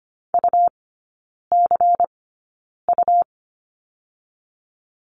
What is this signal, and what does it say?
Morse code, heard as a keyed tone, about 25 wpm, spaced as for 9 wpm, 700 Hz. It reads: UCU
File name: Lesson 17e - Letter u - Three Characters 25wpm